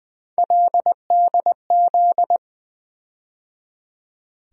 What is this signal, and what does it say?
Morse code, heard as a keyed tone, about 21 wpm, 700 Hz. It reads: LDZ
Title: USA Call Sign Postfixes - 3 Letters 20wpm